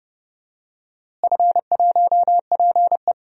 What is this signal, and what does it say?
Morse code, heard as a keyed tone, about 30 wpm, 700 Hz. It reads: F1PE
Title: DXCC-100 Call Signs - 4 Characters - Repeated 2x - 30wpm